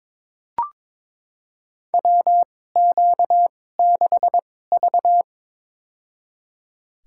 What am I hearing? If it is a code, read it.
Morse code, heard as a keyed tone, about 22 wpm, spaced as for 16 wpm, 700 Hz. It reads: WQ6V
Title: USA Call Signs - 4 Characters - 2x Character Spacing - 22wpm